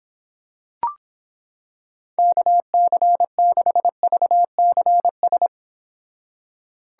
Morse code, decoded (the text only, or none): KC6VCS